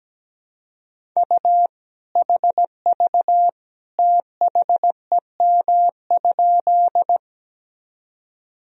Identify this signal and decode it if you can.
Morse code, heard as a keyed tone, about 17 wpm, 700 Hz. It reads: U HV THEM?
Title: Sentences from Top 100 Words and Abbreviations 17wpm